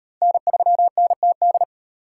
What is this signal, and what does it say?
Morse code, heard as a keyed tone, about 31 wpm, 700 Hz. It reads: N3NTD
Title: Speed Racing - USA 1x3 Call Signs 25wpm